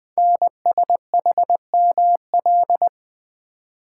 Morse code, heard as a keyed tone, about 20 wpm, 700 Hz. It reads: NSHML
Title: Code Groups - 5 Letters - 20wpm